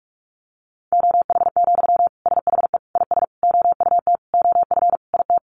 Morse code, heard as a keyed tone, about 44 wpm, 700 Hz. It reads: OH, SHE IS OUT OF IT
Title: Sentences from Top 1100 Words 45wpm